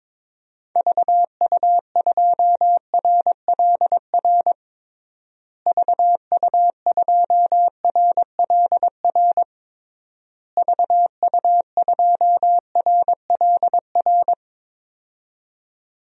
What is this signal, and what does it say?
Morse code, heard as a keyed tone, about 21 wpm, 700 Hz. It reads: VU2RLR VU2RLR VU2RLR
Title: DXCC-100 Call Signs - 6 Characters - Repeated 3x - 22wpm